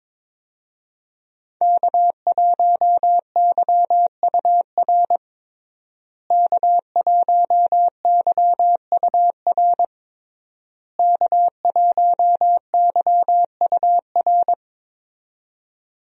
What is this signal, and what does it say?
Morse code, heard as a keyed tone, about 22 wpm, 700 Hz. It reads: K1YUR K1YUR K1YUR
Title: USA Call Signs - 5 Characters - Repeated 3x - 22wpm